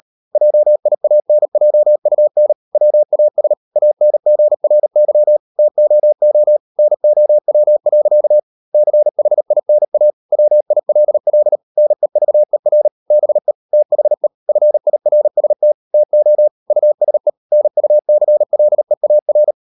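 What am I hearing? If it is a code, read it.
Morse code, heard as a keyed tone, about 40 wpm, 600 Hz. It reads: JIANJUN WAS ANGRY TOO NOW. CHINA WILL NEVER BE THE FIRST TO USE NUCLEAR
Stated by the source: Morse code practice at 35–39 wpm sending English text of Ken Follett's novel Never, from CW med Gnister.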